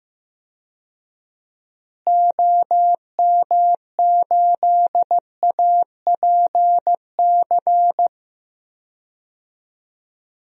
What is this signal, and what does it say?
Morse code, heard as a keyed tone, about 16 wpm, 700 Hz. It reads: OM8APC
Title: DXCC-100 Call Signs - 6 Characters - Repeated 3x - 15wpm